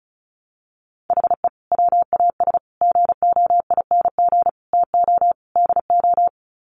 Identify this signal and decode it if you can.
Morse code, heard as a keyed tone, about 34 wpm, 700 Hz. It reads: HE WAS GOING TO DO
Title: Sets of 5 Words Encore2 35wpm